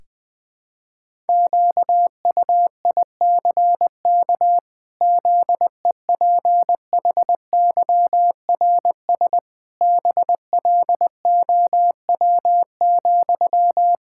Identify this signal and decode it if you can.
Morse code, heard as a keyed tone, about 21 wpm, 700 Hz. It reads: QUICK ZEPHYRS BLOW,
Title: Pangram Challenge v2 - 20wpm